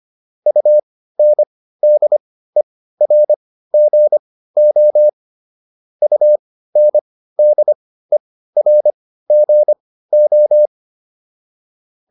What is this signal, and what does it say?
Morse code, heard as a keyed tone, about 25 wpm, spaced as for 15 wpm, 600 Hz. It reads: UNDERGO UNDERGO